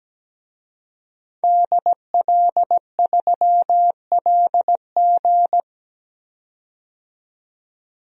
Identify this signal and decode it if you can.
Morse code, heard as a keyed tone, about 17 wpm, 700 Hz. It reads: DL3LG
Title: DXCC-100 Call Signs - 5 Characters - 17wpm